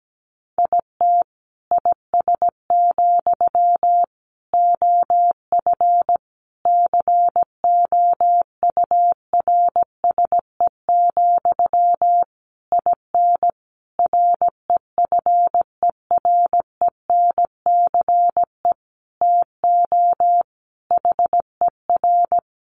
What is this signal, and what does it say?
Morse code, heard as a keyed tone, about 17 wpm, 700 Hz. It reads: IT IS, OF COURSE, IN REFERENCE TO HER
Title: Sentences from Top 1200 Words 17wpm